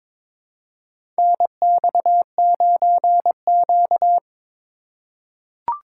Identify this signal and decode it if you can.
Morse code, heard as a keyed tone, about 22 wpm, 700 Hz. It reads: NX9Q
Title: USA Call Signs - 4x Character Spacing - 22wpm